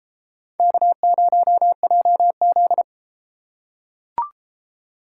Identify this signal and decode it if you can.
Morse code, heard as a keyed tone, about 31 wpm, 700 Hz. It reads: K0JZ